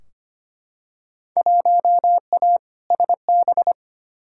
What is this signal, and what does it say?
Morse code, heard as a keyed tone, about 25 wpm, 700 Hz. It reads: 1A SB
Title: Field Day Exchanges Encore - 25wpm